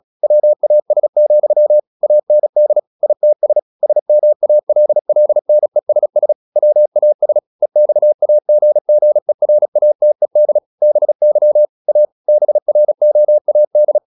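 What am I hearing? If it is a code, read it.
Morse code, heard as a keyed tone, about 36 wpm, 600 Hz. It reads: WAS, AND ITS SMALLNESS WAS EXAGGERATED BY A BROAD
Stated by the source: Morse practice at 36 wpm, text of Ken Follett's novel Never, from CW med Gnister.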